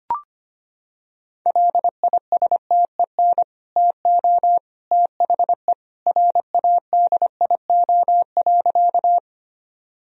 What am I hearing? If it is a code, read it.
Morse code, heard as a keyed tone, about 25 wpm, 700 Hz. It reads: LISTEN TO THE RADIO.
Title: Sentences from Top 1000 Words - Review Part 3 - 25wpm